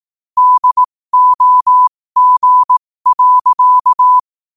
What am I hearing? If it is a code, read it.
Morse code, heard as a keyed tone, about 18 wpm, spaced as for 15 wpm, 1000 Hz. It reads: DOG.
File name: qotdfiles / 15wpm